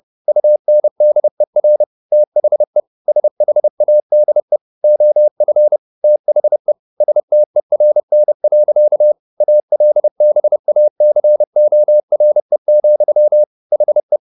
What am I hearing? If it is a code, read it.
Morse code, heard as a keyed tone, about 30 wpm, 600 Hz. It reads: UNDER THE SHADE OF THE STERN. ALBACORE, HE
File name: the old man and the sea / the old man and the sea 1 30wpm